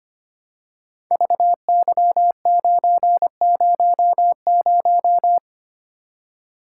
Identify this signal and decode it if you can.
Morse code, heard as a keyed tone, about 25 wpm, 700 Hz. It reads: VY900
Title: Code Groups - 5 Letter:Numbers - 25wpm